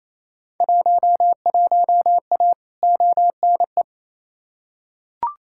Morse code, heard as a keyed tone, about 28 wpm, 700 Hz. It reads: 11A ONE
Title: Field Day Exchanges Encore - 28wpm